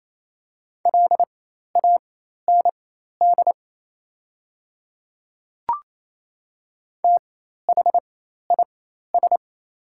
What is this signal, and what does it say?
Morse code, heard as a keyed tone, about 27 wpm, spaced as for 13 wpm, 690 Hz. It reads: LAND THIS
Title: 4x Character Spacing - Top 200 Words - 28wpm